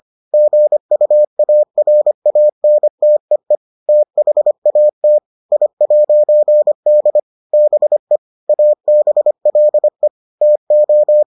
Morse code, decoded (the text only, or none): GUARANTEE THAT I'D BE ABLE TO